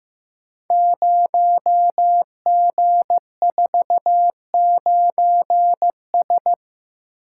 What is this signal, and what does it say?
Morse code, heard as a keyed tone, about 15 wpm, 700 Hz. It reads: 0G49S
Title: Code Groups - 5 Letter:Numbers - 15wpm